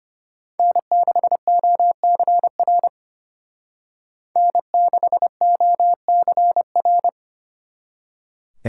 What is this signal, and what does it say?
Morse code, heard as a keyed tone, about 33 wpm, 700 Hz. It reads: N6OCR N6OCR